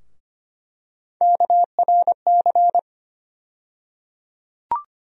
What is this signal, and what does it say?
Morse code, heard as a keyed tone, about 25 wpm, 700 Hz. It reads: KRC
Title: USA Call Sign Postfixes - 3 Letters 25wpm